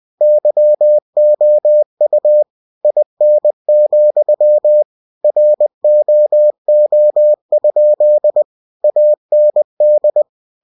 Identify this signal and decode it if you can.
Morse code, heard as a keyed tone, about 20 wpm, 600 Hz. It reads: YOU IN, ROO? AND
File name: cw-books / winnie-the-pooh / winnie-20wpm-page-0069